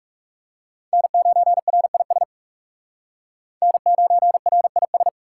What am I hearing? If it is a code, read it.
Morse code, heard as a keyed tone, about 37 wpm, 700 Hz. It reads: N9RIS N9RIS
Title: Speed Racing - USA Call Signs 30wpm